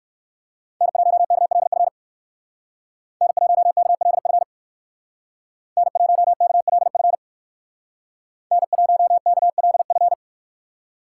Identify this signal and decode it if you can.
Morse code, heard as a keyed tone, about 56 wpm, 700 Hz. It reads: N1KLF N1KLF N1KLF N1KLF
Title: Speed Racing - USA 1x3 Call Signs 45wpm